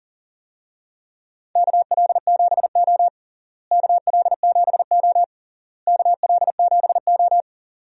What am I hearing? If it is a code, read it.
Morse code, heard as a keyed tone, about 40 wpm, 700 Hz. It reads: KL7O KL7O KL7O
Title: DXCC-100 Call Signs - 4 Characters - Repeated 3x - 40wpm